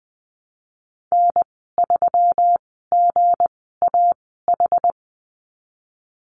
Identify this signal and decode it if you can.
Morse code, heard as a keyed tone, about 20 wpm, spaced as for 15 wpm, 700 Hz. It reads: N3GAH
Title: 161 Lesson 26g - Letter g - Call Signs 20wpm